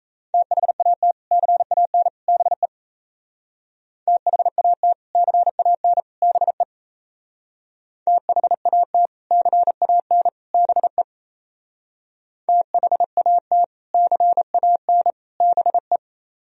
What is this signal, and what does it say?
Morse code, heard as a keyed tone, about 34 wpm, 700 Hz. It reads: THAT CAN BE THAT CAN BE THAT CAN BE THAT CAN BE
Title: Speed-Racing - Sets of 3 Words 28wpm